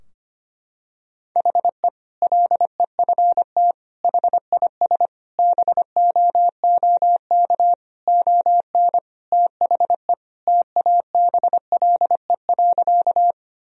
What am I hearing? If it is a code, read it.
Morse code, heard as a keyed tone, about 25 wpm, 700 Hz. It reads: HE LEFT HIS BOOK ON THE TABLE.
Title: Sentences from Top 1000 Words - Review Part 2 - 25wpm